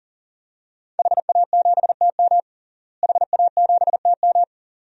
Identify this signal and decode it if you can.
Morse code, heard as a keyed tone, about 40 wpm, 700 Hz. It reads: HA7TM HA7TM
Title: DXCC-100 Call Signs - 5 Characters - Repeated 2x - 40wpm